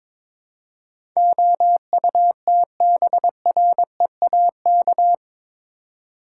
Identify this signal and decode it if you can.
Morse code, heard as a keyed tone, about 21 wpm, 700 Hz. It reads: OUTBREAK
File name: practice / 22wpm / Eight Letter Words - 22wpm